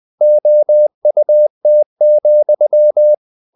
Morse code, heard as a keyed tone, about 20 wpm, 600 Hz. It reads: OUT,
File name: the old man and the sea / the old man and the sea part 1 20wpm